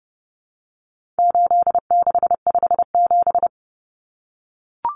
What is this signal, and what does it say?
Morse code, heard as a keyed tone, about 30 wpm, 700 Hz. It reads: 8657
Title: Common Cut Numbers - 4-Digit 30wpm